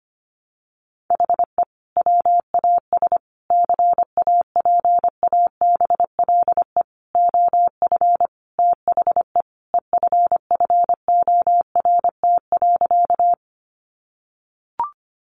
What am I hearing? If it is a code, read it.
Morse code, heard as a keyed tone, about 25 wpm, 700 Hz. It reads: HE WAS CAPABLE OF THE EFFORT.
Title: Sentences from Top 1200 Words 25wpm